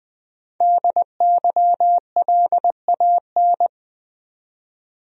Morse code, heard as a keyed tone, about 20 wpm, 700 Hz. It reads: DYLAN
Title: USA Common Names - 20wpm